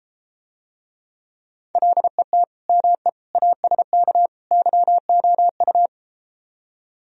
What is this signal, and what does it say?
Morse code, heard as a keyed tone, about 33 wpm, 700 Hz. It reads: LET ME ASK YOU